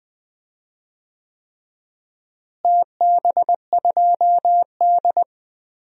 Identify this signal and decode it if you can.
Morse code, heard as a keyed tone, about 20 wpm, 700 Hz. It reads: TB2D
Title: DXCC-100 Call Signs - 4 Characters - Repeated 3x - 20wpm